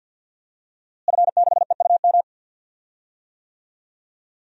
Morse code, heard as a keyed tone, about 53 wpm, 700 Hz. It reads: F6EUM
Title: DXCC-100 Call Signs - 5 Characters - Repeated 2x - 50wpm